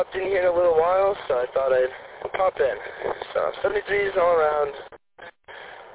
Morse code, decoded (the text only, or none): none